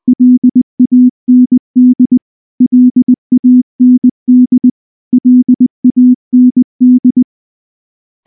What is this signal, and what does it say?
Morse code, heard as a keyed tone, about 19 wpm, 250 Hz. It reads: LAND LAND LAND
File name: morse-training-20WPM